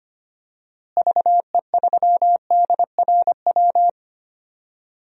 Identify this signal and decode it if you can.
Morse code, heard as a keyed tone, about 25 wpm, 700 Hz. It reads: VE3DRW
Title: DXCC-100 Call Signs - 6 Characters - Repeated 3x - 25wpm